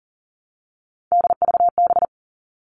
Morse code, heard as a keyed tone, about 40 wpm, 700 Hz. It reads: DVB
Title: All Cut Numbers - 3-Digit 40wpm